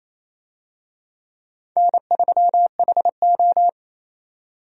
Morse code, heard as a keyed tone, about 28 wpm, 700 Hz. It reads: N3HO
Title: USA Call Signs - 4 Characters - 28wpm